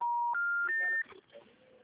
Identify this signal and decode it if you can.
Special information tone.
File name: out- -2021- - - .65477